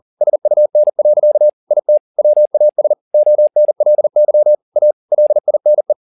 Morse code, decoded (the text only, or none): SUN. IT WAS ONLY A LINE